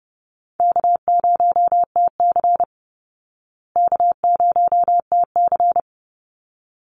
Morse code, heard as a keyed tone, about 30 wpm, 700 Hz. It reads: K0TC K0TC
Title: Call Signs Repeated 3x - 30wpm